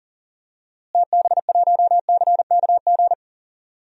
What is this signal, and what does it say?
Morse code, heard as a keyed tone, about 40 wpm, 700 Hz. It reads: TB1CKG
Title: DXCC-100 Call Signs - 6 Characters - Repeated 3x - 40wpm